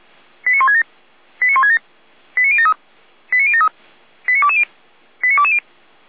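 Five-tone selective call (ZVEI), heard as quick runs of five tones, tones of about 80 ms. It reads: 89167 89167 89073 89073 89220 89220
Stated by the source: ZVEI five-tone selective call received by SDR.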